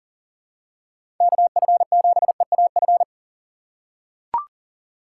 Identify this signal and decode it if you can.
Morse code, heard as a keyed tone, about 40 wpm, 700 Hz. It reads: KF7EAF
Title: USA Call Signs - 6 Characters - 6x Character Spacing - 40wpm